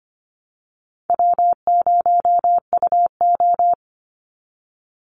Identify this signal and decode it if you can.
Morse code, heard as a keyed tone, about 25 wpm, 700 Hz. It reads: W0UO
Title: Call Signs Repeated 2x - 25wpm